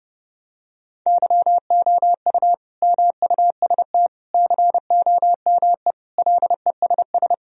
Morse code, heard as a keyed tone, about 30 wpm, 700 Hz. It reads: YOU MUST COME LESS
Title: Sentences from Top 1000 Words - Review Part 2 - 30wpm